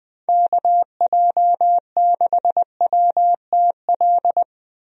Morse code, heard as a keyed tone, about 20 wpm, 700 Hz. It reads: KJ6WTL
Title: USA Call Signs - 6 Characters - Repeated 3x - 20wpm